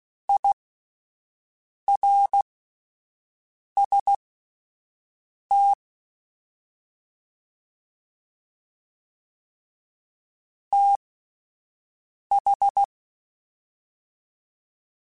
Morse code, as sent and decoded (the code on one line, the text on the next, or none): .. .-. ... - / - ....
IRST TH